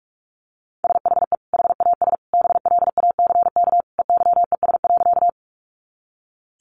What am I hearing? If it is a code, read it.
Morse code, heard as a keyed tone, about 44 wpm, 700 Hz. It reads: SHE HAS BLACK EYES.
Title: Easy Phrases 45wpm